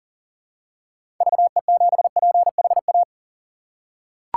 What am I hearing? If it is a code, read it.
Morse code, heard as a keyed tone, about 40 wpm, 700 Hz. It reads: VE7PHA
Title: K1USN SST QSO Practice - 40wpm